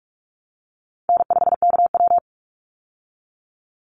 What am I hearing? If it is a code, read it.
Morse code, heard as a keyed tone, about 44 wpm, 700 Hz. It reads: N5KW
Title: Easy Call Signs Repeated 3x - 45wpm